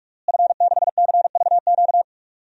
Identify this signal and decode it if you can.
Morse code, heard as a keyed tone, about 46 wpm, 700 Hz. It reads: F6CVQ
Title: DXCC-100 Call Signs - 5 Characters - Repeated 3x - 45wpm